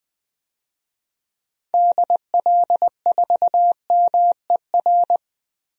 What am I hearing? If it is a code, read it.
Morse code, heard as a keyed tone, about 20 wpm, 700 Hz. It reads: DL4MER
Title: DXCC-100 Call Signs - 6 Characters - Repeated 2x - 20wpm